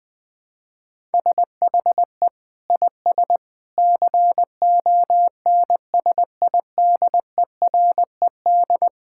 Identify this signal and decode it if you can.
Morse code, heard as a keyed tone, about 20 wpm, 700 Hz. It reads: SHE IS CONSIDERED
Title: Sentences with 8-Word Phrases - 20wpm